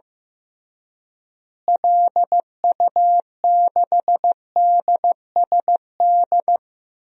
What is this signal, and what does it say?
Morse code, heard as a keyed tone, about 15 wpm, 700 Hz. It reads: LU6DSD
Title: DXCC-100 Call Signs - 6 Characters - Repeated 3x - 15wpm